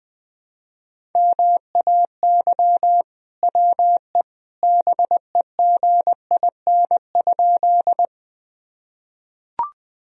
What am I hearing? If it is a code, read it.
Morse code, heard as a keyed tone, about 21 wpm, 700 Hz. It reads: MAY WE BEGIN?
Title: Sentences from Top 1000 Words - Review Part 1 - 20wpm